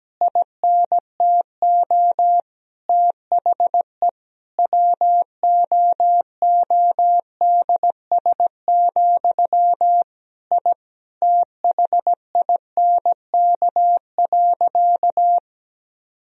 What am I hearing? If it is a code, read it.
Morse code, heard as a keyed tone, about 17 wpm, 700 Hz. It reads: INTO THE WOODS, I THINK.
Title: Sentences from Top 1000 Words - Review Part 2 - 17wpm